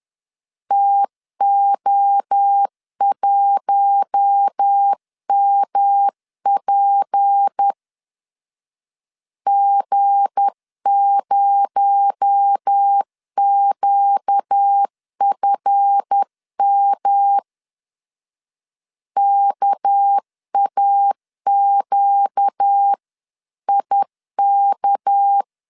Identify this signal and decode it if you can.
Morse code, heard as a keyed tone, about 10 wpm, 800 Hz. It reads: TO1MP G0QFM KAQIK
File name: LEZIONI / 7 GF9Z / 7 GF 10wpm 2514